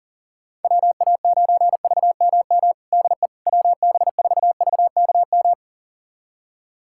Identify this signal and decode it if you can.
Morse code, heard as a keyed tone, about 37 wpm, 700 Hz. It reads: WA9VMM DE WB4VKM
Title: USA Callsign DE Callsign - 6 Characters - 40wpm